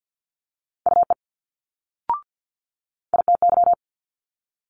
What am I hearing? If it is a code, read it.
Morse code, heard as a keyed tone, about 50 wpm, 700 Hz. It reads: UE ITY
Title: Multi-letter Phonemes Quick Fire 50wpm